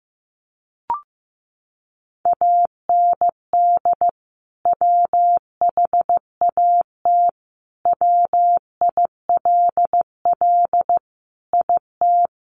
Morse code, heard as a keyed tone, about 15 wpm, 700 Hz. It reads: AND WHAT WILL IT
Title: Sentences from Top 1200 Words 15wpm